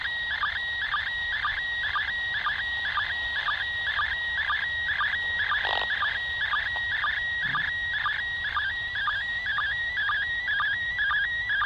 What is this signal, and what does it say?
SSTV (slow-scan television), PD 120.
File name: ISS 2017-12-06 145 800MHz SSTV